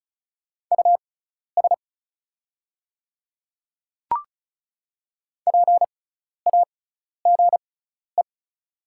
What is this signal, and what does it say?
Morse code, heard as a keyed tone, about 34 wpm, spaced as for 12 wpm, 690 Hz. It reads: US PAGE
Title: 6x Character Spacing - Top 500 Words - 35wpm